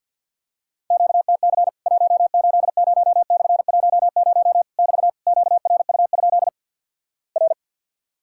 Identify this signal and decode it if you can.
Morse code, heard as a keyed tone, about 50 wpm, 700 Hz. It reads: QTC 180/10 <BT> QRU? R
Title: Worked All Europe DX Contest - QTC Exchange - 50wpm